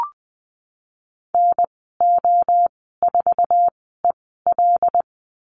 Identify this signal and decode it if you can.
Morse code, heard as a keyed tone, about 20 wpm, 700 Hz. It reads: N O 4 E L
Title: 076 Lesson 13g - Letter L - Call Signs 20wpm